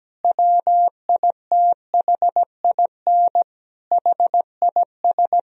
Morse code, heard as a keyed tone, about 17 wpm, 700 Hz. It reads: WITHIN HIS
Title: Sentences from Top 1000 Words - Review Part 4 - 17wpm